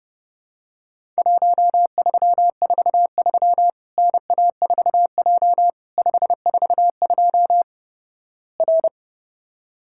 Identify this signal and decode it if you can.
Morse code, heard as a keyed tone, about 30 wpm, 700 Hz. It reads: 1343 NA4J 542 R
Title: Worked All Europe DX Contest - QTC Exchange - 30wpm